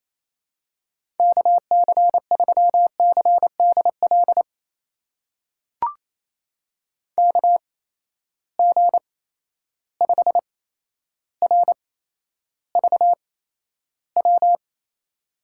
Morse code, heard as a keyed tone, about 28 wpm, spaced as for 12 wpm, 700 Hz. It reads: KC3CDL KG5RVW
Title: USA Call Signs - 6 Characters - 8x Character Spacing - 28wpm